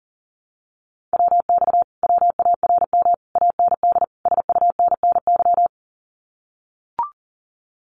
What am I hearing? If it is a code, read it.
Morse code, heard as a keyed tone, about 40 wpm, 700 Hz. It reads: WX WARM AND SUNNY
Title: Common QSO Phrases Encore 40wpm